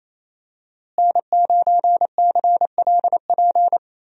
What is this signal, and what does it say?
Morse code, heard as a keyed tone, about 28 wpm, 700 Hz. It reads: N9CLP
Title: USA Call Signs - 5 Characters - Repeated 2x - 28wpm